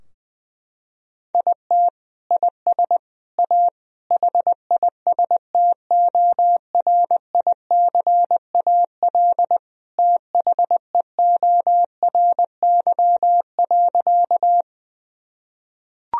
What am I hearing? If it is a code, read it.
Morse code, heard as a keyed tone, about 20 wpm, 700 Hz. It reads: IT IS A HISTORICAL THEORY.
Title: Sentences from Top 1000 Words - Review Part 4 - 20wpm